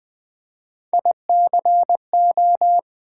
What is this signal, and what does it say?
Morse code, heard as a keyed tone, about 20 wpm, 700 Hz. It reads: ICO
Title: Build A Word - 1 Letter At A Time - 4-Letter Words - 20wpm